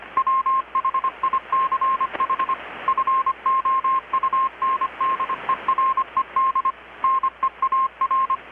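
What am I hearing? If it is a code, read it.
Morse code, heard as a keyed tone, about 25 wpm, 1060 Hz, with radio noise. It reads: WHICH FOUNDERED NEAR
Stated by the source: Morse from KPH's Night of Nights XX maritime radiotelegraphy event, received in Chiapas, Mexico about 3,800 km away on a Kenwood R-600 with a grounded 10 m long-wire antenna.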